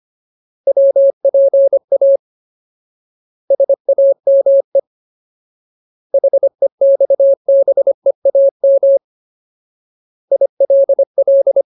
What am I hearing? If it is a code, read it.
Morse code, heard as a keyed tone, about 25 wpm, 550 Hz. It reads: WPA SAME HEXBEAM ILL
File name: Ultimate Word List / 25WPM Ultimate